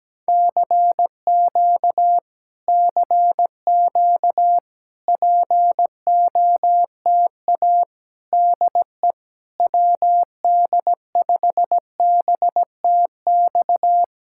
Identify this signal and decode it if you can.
Morse code, heard as a keyed tone, about 17 wpm, 700 Hz. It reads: CQ CQ POTA DE WD5BTX
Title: Speed-Racing Common QSO Phrases Encore - 17wpm